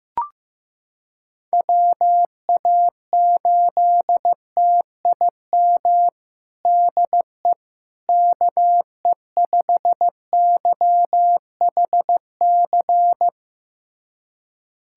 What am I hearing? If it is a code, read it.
Morse code, heard as a keyed tone, about 16 wpm, 700 Hz. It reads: WA8TIM DE KE5YHC
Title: USA Callsign DE Callsign 15wpm